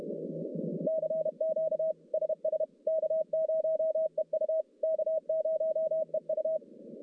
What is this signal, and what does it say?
Morse code, heard as a keyed tone, about 31 wpm, 640 Hz. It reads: CQ SS K0EU K0EU
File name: contests / CW-SS-k5tr- -1200